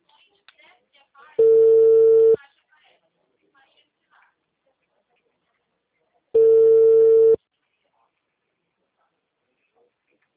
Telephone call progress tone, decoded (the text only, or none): ringback tone